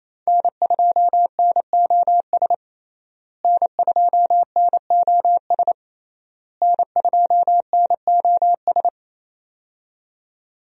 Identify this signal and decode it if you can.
Morse code, heard as a keyed tone, about 28 wpm, 700 Hz. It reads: N2NOS N2NOS N2NOS
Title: USA Call Signs - 5 Characters - Repeated 3x - 28wpm